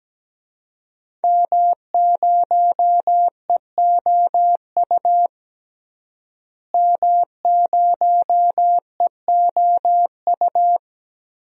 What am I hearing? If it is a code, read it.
Morse code, heard as a keyed tone, about 17 wpm, 700 Hz. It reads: M0EOU M0EOU
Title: DXCC-100 Call Signs - 5 Characters - Repeated 3x - 17wpm